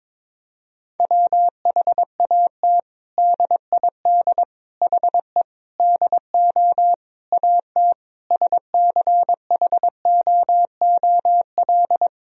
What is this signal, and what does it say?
Morse code, heard as a keyed tone, about 22 wpm, 700 Hz. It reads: WHAT DID HE DO AT SCHOOL
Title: Sentences from Top 1000 Words - Review Part 1 - 22wpm